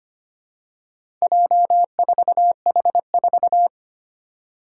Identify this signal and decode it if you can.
Morse code, heard as a keyed tone, about 25 wpm, 700 Hz. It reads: J4H4
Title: Four Challenging Letter-Number Combinations 25wpm